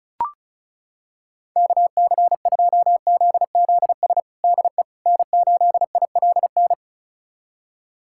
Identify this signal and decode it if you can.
Morse code, heard as a keyed tone, about 36 wpm, 700 Hz. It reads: KC2ZZS DE N8ILN
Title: USA Callsign DE Callsign 35wpm